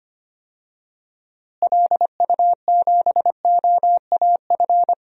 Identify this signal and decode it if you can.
Morse code, heard as a keyed tone, about 25 wpm, 700 Hz. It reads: LU7OAF
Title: DXCC-100 Call Signs - 6 Characters - Repeated 2x - 25wpm